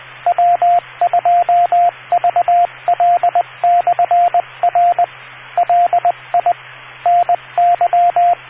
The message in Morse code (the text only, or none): W2VL/R LI NY